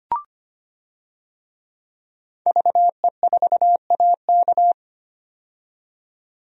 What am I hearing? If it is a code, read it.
Morse code, heard as a keyed tone, about 25 wpm, 700 Hz. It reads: VE4AK
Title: DXCC-100 Call Signs - 5 Characters - 25wpm